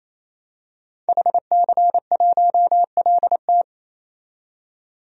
Morse code, heard as a keyed tone, about 28 wpm, 700 Hz. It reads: HC1LT
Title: DXCC-100 Call Signs - 5 Characters - 28wpm